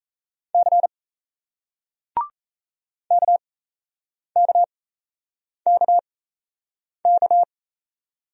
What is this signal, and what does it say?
Morse code, heard as a keyed tone, about 38 wpm, spaced as for 8 wpm, 700 Hz. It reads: C KKKK